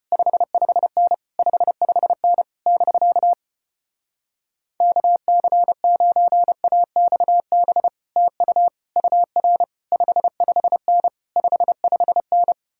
Morse code, heard as a keyed tone, about 33 wpm, 700 Hz. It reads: 55N 55N <BK> KC9AXB TU UR 55N 55N